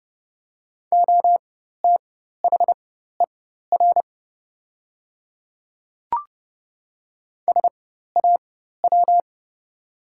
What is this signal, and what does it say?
Morse code, heard as a keyed tone, about 30 wpm, spaced as for 14 wpm, 700 Hz. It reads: OTHER SAW
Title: 4x Character Spacing - Top 300 Words - 30wpm